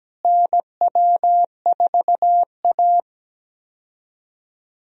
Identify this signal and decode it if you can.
Morse code, heard as a keyed tone, about 17 wpm, 700 Hz. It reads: NW4A
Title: USA Call Signs - 4 Characters - Repeated 3x - 17wpm